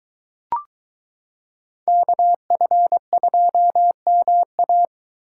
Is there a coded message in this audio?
Yes: Morse code from 1.9 to 4.9 s, heard as a keyed tone, about 22 wpm, 700 Hz. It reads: KF2MA